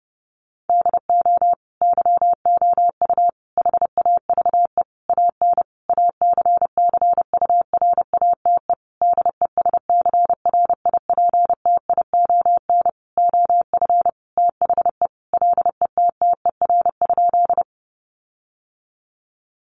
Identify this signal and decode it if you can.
Morse code, heard as a keyed tone, about 30 wpm, 700 Hz. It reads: DO YOU HAVE AN ACCURATE DESCRIPTION OF THE LETTER?
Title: Sentences from Top 1300 Words 30wpm